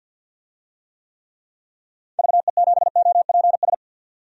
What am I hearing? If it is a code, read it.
Morse code, heard as a keyed tone, about 45 wpm, 700 Hz. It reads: VE7OPS
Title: DXCC-100 Call Signs - 6 Characters - 50wpm